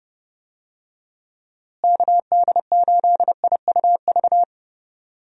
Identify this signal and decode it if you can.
Morse code, heard as a keyed tone, about 30 wpm, 700 Hz. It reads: KD8IUV
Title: USA Call Signs - 6 Characters - Repeated 2x - 30wpm